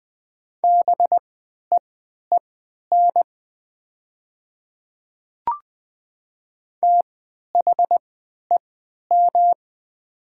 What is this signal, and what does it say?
Morse code, heard as a keyed tone, about 20 wpm, spaced as for 11 wpm, 700 Hz. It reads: BEEN THEM
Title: 3x Character Spacing - Top 100 Words - 20wpm